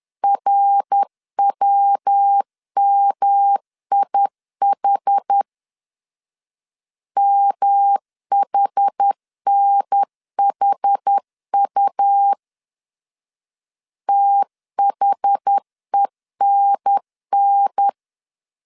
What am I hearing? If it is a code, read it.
Morse code, heard as a keyed tone, about 10 wpm, 800 Hz. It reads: RWMIH MHNHU THENN
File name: CW / DET / 1 3 1500C S B 10wpm 3335